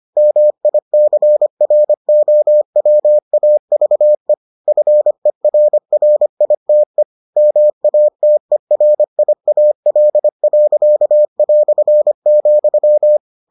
Morse code, heard as a keyed tone, about 25 wpm, 600 Hz. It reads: MICROWAVE FERRITE MATERIAL.",